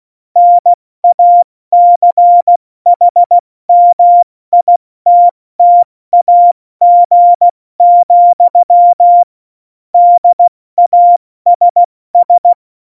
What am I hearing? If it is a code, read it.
Morse code, heard as a keyed tone, about 16 wpm, 700 Hz. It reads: NACHMITTAG, DASS
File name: Morsekurs / Witze / Witz 17. November 2020, 70 F80